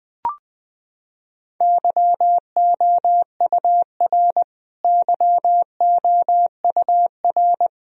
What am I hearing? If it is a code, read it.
Morse code, heard as a keyed tone, about 21 wpm, 700 Hz. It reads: YOUR YOUR
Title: Top 100 Words Repeated 20wpm